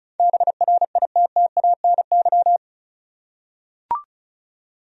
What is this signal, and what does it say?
Morse code, heard as a keyed tone, about 36 wpm, 700 Hz. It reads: BRITTANY
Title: USA Common Names - 35wpm